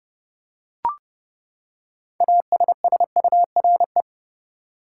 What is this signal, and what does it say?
Morse code, heard as a keyed tone, about 32 wpm, 700 Hz. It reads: ASSURE
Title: Six Letter Words - 30wpm